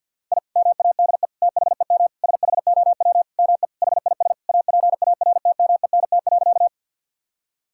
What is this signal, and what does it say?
Morse code, heard as a keyed tone, about 50 wpm, 700 Hz. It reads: I MADE THEM SHOW ME HIS APARTMENT.
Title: Sentences from Top 1000 Words - Review Part 6 - 50wpm